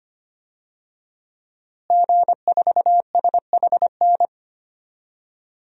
Morse code, heard as a keyed tone, about 25 wpm, 700 Hz. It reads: G4SHN
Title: DXCC-100 Call Signs - 5 Characters - Repeated 2x - 25wpm